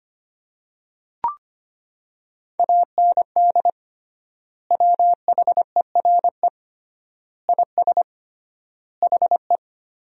Morse code, heard as a keyed tone, about 25 wpm, 700 Hz. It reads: AND WHERE IS HE
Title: 3x Word-Spacing - Sentences from Top 200 Words - 25wpm